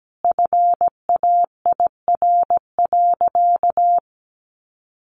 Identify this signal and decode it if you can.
Morse code, heard as a keyed tone, about 17 wpm, 700 Hz. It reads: FAIR.
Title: Sentences from Top 1300 Words 17wpm